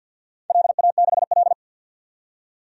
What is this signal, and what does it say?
Morse code, heard as a keyed tone, about 48 wpm, 700 Hz. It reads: LA6L
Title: DXCC-100 Call Signs - 4 Characters - 50wpm